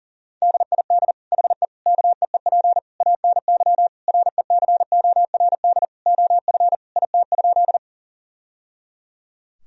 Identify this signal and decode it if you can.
Morse code, heard as a keyed tone, about 40 wpm, 700 Hz. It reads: DID HE KEEP ANY RECORD OF IT?
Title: Sentences from Top 1000 Words - Review Part 3 - 40wpm